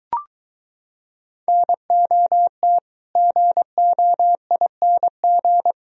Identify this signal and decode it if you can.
Morse code, heard as a keyed tone, about 23 wpm, 700 Hz. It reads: NOT GOING